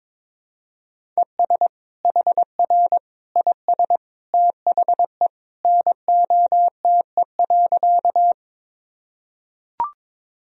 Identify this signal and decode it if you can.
Morse code, heard as a keyed tone, about 22 wpm, 700 Hz. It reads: ES HR IS THE NOTE.
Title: Sentences from Top 500 Words and Abbreviations 22wpm